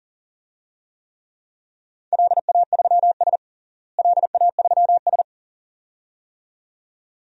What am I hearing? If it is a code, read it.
Morse code, heard as a keyed tone, about 37 wpm, 700 Hz. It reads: LA3S LA3S
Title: DXCC-100 Call Signs - 4 Characters - Repeated 2x - 40wpm